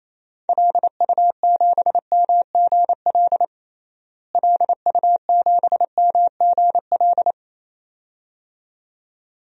Morse code, heard as a keyed tone, about 28 wpm, 700 Hz. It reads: LU7MGL LU7MGL
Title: DXCC-100 Call Signs - 6 Characters - Repeated 2x - 28wpm